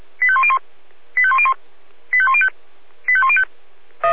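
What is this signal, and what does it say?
Five-tone selective call (ZVEI), heard as quick runs of five tones, tones of about 70 ms. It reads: 85101 85101 85105 85105
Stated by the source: Five-tone ZVEI selective call 85100 received on an SDR.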